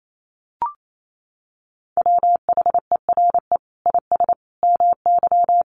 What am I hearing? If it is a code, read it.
Morse code, heard as a keyed tone, about 28 wpm, 700 Hz. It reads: WHERE IS MY
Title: Sentences from Top 1400 Words 28wpm